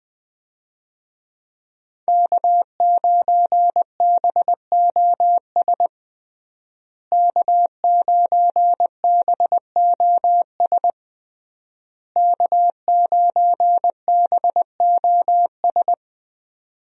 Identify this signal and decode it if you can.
Morse code, heard as a keyed tone, about 20 wpm, 700 Hz. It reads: K9BOS K9BOS K9BOS
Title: USA Call Signs - 5 Characters - Repeated 3x - 20wpm